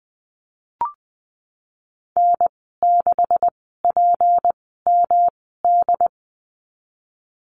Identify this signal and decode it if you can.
Morse code, heard as a keyed tone, about 20 wpm, spaced as for 15 wpm, 700 Hz. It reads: N6PMD
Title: 154 Lesson 25g - Letter p - Call Signs 20wpm